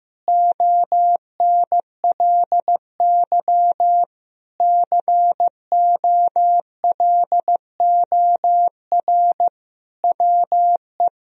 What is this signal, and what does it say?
Morse code, heard as a keyed tone, about 15 wpm, 700 Hz. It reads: ONLY COLOR WE
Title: Sentences from Top 1000 Words - Review Part 3 - 15wpm